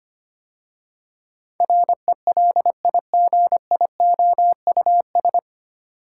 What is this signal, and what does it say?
Morse code, heard as a keyed tone, about 25 wpm, 700 Hz. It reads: RELIGIOUS
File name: practice / 25wpm / 9 Letter Words - First 3 Letters Given - 25wpm